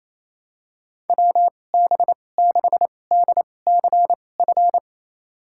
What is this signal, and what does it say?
Morse code, heard as a keyed tone, about 28 wpm, spaced as for 20 wpm, 690 Hz. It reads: WB6DCF
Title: USA Call Signs - 2x Character Spacing - 28wpm